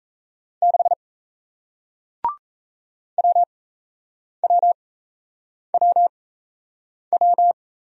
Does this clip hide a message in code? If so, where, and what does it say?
Yes: Morse code from 0.6 to 7.5 s, heard as a keyed tone, about 38 wpm, spaced as for 8 wpm, 700 Hz. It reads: 6 WWWW